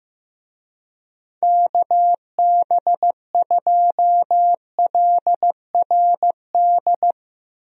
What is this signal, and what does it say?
Morse code, heard as a keyed tone, about 15 wpm, 700 Hz. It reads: KB2LRD
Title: USA Call Signs - 6 Characters - Repeated 3x - 15wpm